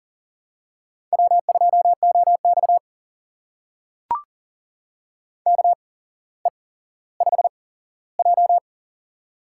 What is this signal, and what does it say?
Morse code, heard as a keyed tone, about 38 wpm, spaced as for 17 wpm, 700 Hz. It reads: W2OX KE5J
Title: USA Call Signs - 4 Characters - 8x Character Spacing - 40wpm